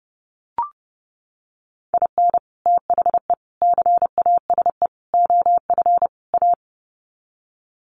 Morse code, heard as a keyed tone, about 30 wpm, 700 Hz. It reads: IN THE CASE OF A
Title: Sets of 5 Words Encore 30wpm